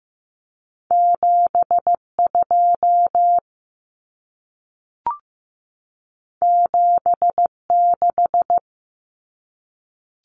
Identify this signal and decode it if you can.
Morse code, heard as a keyed tone, about 15 wpm, 700 Hz. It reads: 72 76
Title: Common Cut Numbers - 2-Digit 15wpm